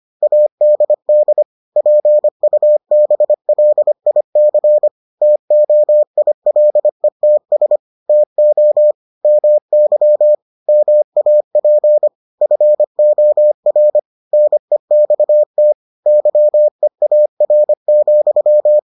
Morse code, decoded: ADD PUBLIC TOILETS TO MY MAP FOR NEXT YEAR,

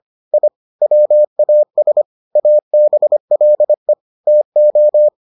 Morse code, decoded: I WAS ABLE TO

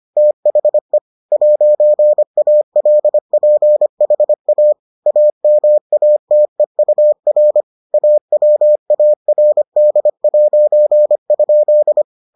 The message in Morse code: THE 'ALPHA AMATEUR AWARD'?

